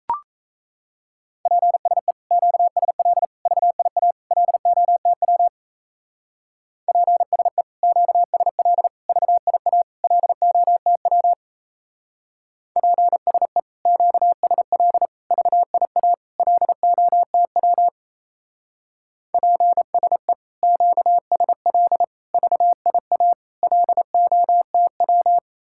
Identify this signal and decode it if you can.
Morse code, heard as a keyed tone, about 35 wpm, 700 Hz. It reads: PSE QSL VIA LOTW PSE QSL VIA LOTW PSE QSL VIA LOTW PSE QSL VIA LOTW